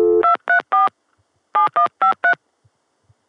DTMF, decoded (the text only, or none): 234 7153